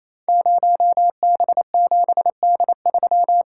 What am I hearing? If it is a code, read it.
Morse code, heard as a keyed tone, about 29 wpm, 700 Hz. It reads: 0B7D3
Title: Code Groups - 5 Letter:Numbers - 28wpm